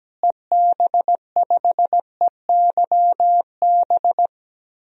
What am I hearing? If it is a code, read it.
Morse code, heard as a keyed tone, about 17 wpm, 700 Hz. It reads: EB5EYB
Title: DXCC-100 Call Signs - 6 Characters - Repeated 2x - 17wpm